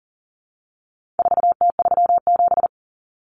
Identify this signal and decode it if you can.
Morse code, heard as a keyed tone, about 40 wpm, 700 Hz. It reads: 4T37
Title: Common Cut Numbers - 4-Digit 40wpm